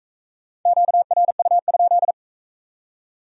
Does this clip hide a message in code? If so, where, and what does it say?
Yes: Morse code from 0.7 to 2.1 s, heard as a keyed tone, about 42 wpm, 700 Hz. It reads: QRU?